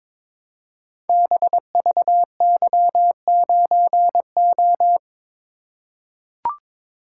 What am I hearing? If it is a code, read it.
Morse code, heard as a keyed tone, about 23 wpm, 700 Hz. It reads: BVY9O
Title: Code Groups - 5 Letter:Numbers - 22wpm